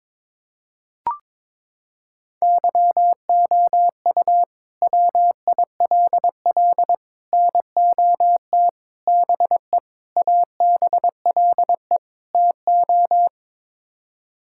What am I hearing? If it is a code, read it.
Morse code, heard as a keyed tone, about 22 wpm, 700 Hz. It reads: YOU WILL NOT BE ABLE TO
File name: practice / 22wpm / Sets of 6 Words - 22wpm